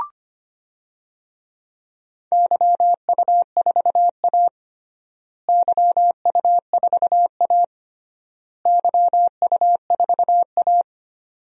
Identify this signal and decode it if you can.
Morse code, heard as a keyed tone, about 25 wpm, 700 Hz. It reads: YU4A YU4A YU4A
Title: DXCC-100 Call Signs - 4 Characters - Repeated 3x - 25wpm